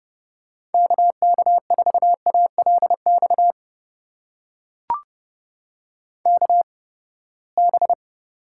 Morse code, heard as a keyed tone, about 32 wpm, spaced as for 21 wpm, 700 Hz. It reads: KK4ALX KB
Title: USA Call Signs - 6 Characters - 8x Character Spacing - 30wpm